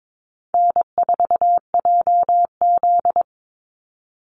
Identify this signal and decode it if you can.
Morse code, heard as a keyed tone, about 22 wpm, 700 Hz. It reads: N4JZ
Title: USA Call Signs 22wpm